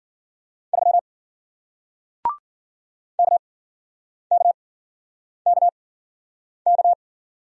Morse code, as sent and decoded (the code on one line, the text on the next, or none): ...-- / -.- -.- -.- -.-
3 KKKK